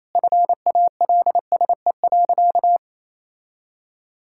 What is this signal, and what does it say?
Morse code, heard as a keyed tone, about 28 wpm, 700 Hz. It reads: FALSE.
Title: Sentences from Top 1600 Words 28wpm